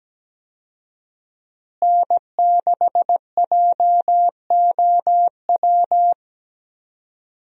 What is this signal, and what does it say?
Morse code, heard as a keyed tone, about 17 wpm, 700 Hz. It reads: N6JOW
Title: USA Call Signs - 5 Characters - Repeated 2x - 17wpm